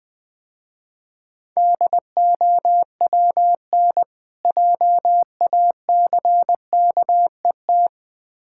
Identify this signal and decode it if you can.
Morse code, heard as a keyed tone, about 22 wpm, 700 Hz. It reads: DOWN JACKET
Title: Difficult Words - Speed-Racing with Hints - 20wpm